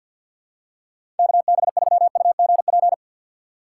Morse code, heard as a keyed tone, about 47 wpm, 700 Hz. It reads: KB3UGP
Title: USA Call Signs - 6 Characters - 3x Character Spacing - 50wpm